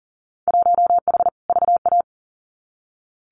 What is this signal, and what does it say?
Morse code, heard as a keyed tone, about 40 wpm, 700 Hz. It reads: 1H VA
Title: Winter Field Day Exchanges 40wpm